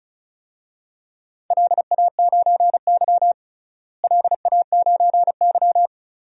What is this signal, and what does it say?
Morse code, heard as a keyed tone, about 36 wpm, 700 Hz. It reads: LA9Y LA9Y
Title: DXCC-100 Call Signs - 4 Characters - Repeated 3x - 35wpm